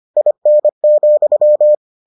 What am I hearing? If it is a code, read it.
Morse code, heard as a keyed tone, about 25 wpm, 600 Hz. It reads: IN,